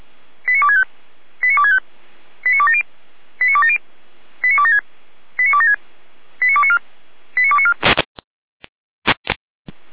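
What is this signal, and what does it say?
Five-tone selective call (ZVEI), heard as quick runs of five tones, tones of about 70 ms. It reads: 89265 89265 89270 89270 89276 89276 89294 89294
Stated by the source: ZVEI five-tone call received by SDR, from the Feuerwehr Orlamünde audio archive.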